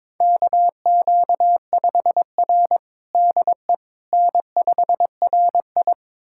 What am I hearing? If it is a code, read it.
Morse code, heard as a keyed tone, about 21 wpm, 700 Hz. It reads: KQ5R DE N5RI
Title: USA Callsign DE Callsign - 4 Characters - 22wpm